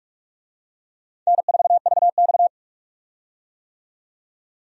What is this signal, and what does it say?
Morse code, heard as a keyed tone, about 46 wpm, 700 Hz. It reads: N4VX
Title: USA Call Signs - 4 Characters - 45wpm